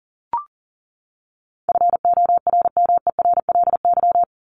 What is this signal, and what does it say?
Morse code, heard as a keyed tone, about 38 wpm, 700 Hz. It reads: FORMERLY
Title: LY Suffix - Top 100 Words with Base Words 40wpm